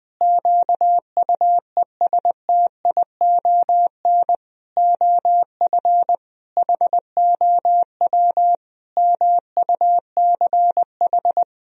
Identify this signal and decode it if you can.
Morse code, heard as a keyed tone, about 19 wpm, 700 Hz. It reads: QUESTION OF HOW MUCH
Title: Sentences from Top 1000 Words - Review Part 3 - 20wpm